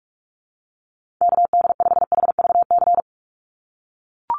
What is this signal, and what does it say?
Morse code, heard as a keyed tone, about 44 wpm, 700 Hz. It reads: KD5HVC
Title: USA Call Signs 45wpm